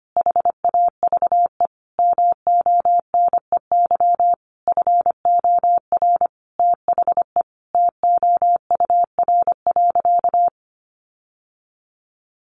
Morse code, heard as a keyed tone, about 25 wpm, 700 Hz. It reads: HAVE MONEY FOR THE TOUR.
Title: Sentences from Top 1400 Words 25wpm